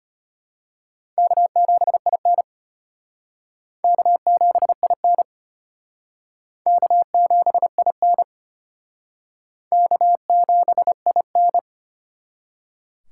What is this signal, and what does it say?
Morse code, heard as a keyed tone, about 31 wpm, 700 Hz. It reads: K7IN K7IN K7IN K7IN